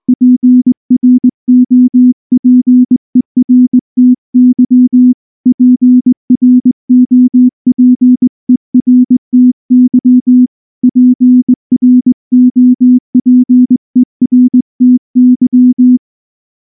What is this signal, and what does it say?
Morse code, heard as a keyed tone, about 20 wpm, 250 Hz. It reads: PROPERTY PROPERTY PROPERTY
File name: morse / morse-training-20WPM-6